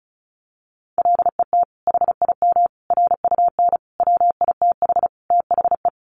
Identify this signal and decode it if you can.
Morse code, heard as a keyed tone, about 34 wpm, 700 Hz. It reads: LET HIM RUN WITH THE
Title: Sentences from Top 1300 Words 35wpm